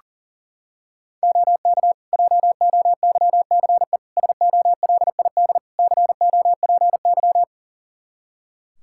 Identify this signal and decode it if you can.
Morse code, heard as a keyed tone, about 40 wpm, 700 Hz. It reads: OK JOYCE SOLID COPY
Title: Practice QSO - Exchange 3 - More Info - 40wpm